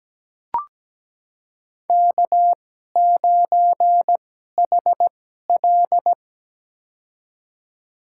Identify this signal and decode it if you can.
Morse code, heard as a keyed tone, about 17 wpm, spaced as for 12 wpm, 700 Hz. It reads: K9HL
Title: USA Call Signs - 4 Characters - 2x Character Spacing - 17wpm